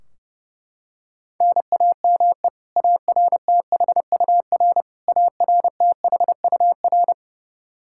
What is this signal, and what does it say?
Morse code, heard as a keyed tone, about 32 wpm, 700 Hz. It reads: NAME ARTHUR ARTHUR
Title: Practice QSO - Exchange 1 - Beginning - 30wpm